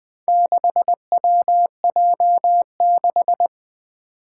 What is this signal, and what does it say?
Morse code, heard as a keyed tone, about 20 wpm, 700 Hz. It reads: 6WJ6
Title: Four Challenging Letter-Number Combinations 20wpm